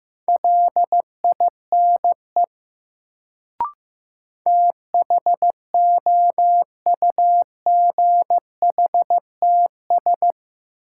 Morse code, heard as a keyed tone, about 15 wpm, 700 Hz. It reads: LINE THOUGHTS
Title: Sentences from Top 1000 Words - Review Part 1 - New Word Drill - 15wpm